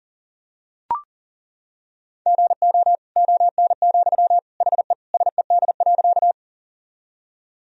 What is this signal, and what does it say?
Morse code, heard as a keyed tone, about 40 wpm, 700 Hz. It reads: GO ON, HE SED.
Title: Sentences from Top 100 Words and Abbreviations 40wpm